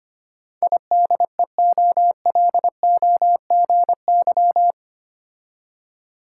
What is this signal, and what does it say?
Morse code, heard as a keyed tone, about 25 wpm, 690 Hz. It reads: IDEOLOGY
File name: 8 Letter Words - First 3 Letters Given - 25wpm